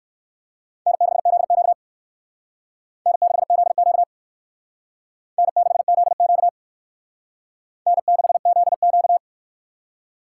Morse code, heard as a keyed tone, about 58 wpm, 700 Hz. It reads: N6ZQ N6ZQ N6ZQ N6ZQ